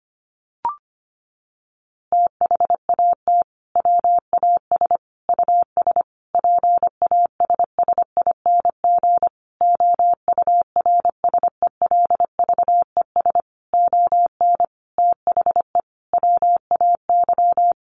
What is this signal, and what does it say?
Morse code, heard as a keyed tone, about 25 wpm, 700 Hz. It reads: THAT WAS US PASSING OURSELVES ON THE WAY
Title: A Sound Of Thunder 25wpm